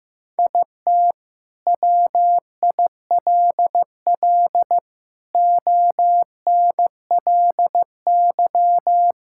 Morse code, heard as a keyed tone, about 15 wpm, 700 Hz. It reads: IT WILL ONLY
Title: Sentences from Top 1000 Words - Review Part 5 - 15wpm